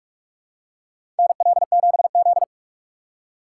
Morse code, heard as a keyed tone, about 46 wpm, 700 Hz. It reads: NL7Z
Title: USA Call Signs - 6x Character Spacing - 45wpm